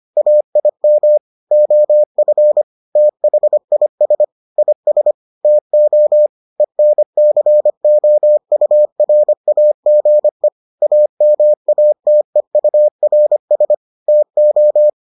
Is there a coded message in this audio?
Yes: Morse code from 0.2 to 14.9 s, heard as a keyed tone, about 25 wpm, 600 Hz. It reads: AIM OF THIS IS TO ENCOURAGE AMATEURS TO